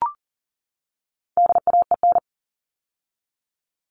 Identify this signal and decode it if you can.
Morse code, heard as a keyed tone, about 40 wpm, 700 Hz. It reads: DAEN
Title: All Cut Numbers - 4-Digit 40wpm